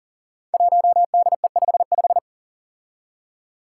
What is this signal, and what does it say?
Morse code, heard as a keyed tone, about 40 wpm, 700 Hz. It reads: 1DE55
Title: Five Challenging Letter-Number Combinations 40wpm